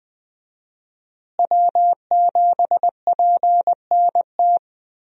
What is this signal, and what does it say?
Morse code, heard as a keyed tone, about 20 wpm, 700 Hz. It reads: W7PNT
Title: USA Call Signs - 5 Characters - Repeated 2x - 20wpm